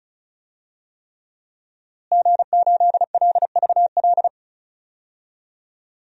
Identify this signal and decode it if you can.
Morse code, heard as a keyed tone, about 34 wpm, 700 Hz. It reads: G8LVL
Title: DXCC-100 Call Signs - 5 Characters - Repeated 3x - 35wpm